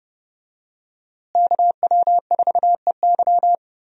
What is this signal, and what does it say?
Morse code, heard as a keyed tone, about 30 wpm, 700 Hz. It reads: KW4EY
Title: USA Call Signs - 5 Characters - Repeated 3x - 30wpm